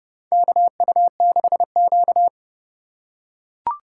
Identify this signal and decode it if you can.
Morse code, heard as a keyed tone, about 30 wpm, 700 Hz. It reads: KU6Q
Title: USA Call Signs - 4 Characters - 3x Character Spacing - 30wpm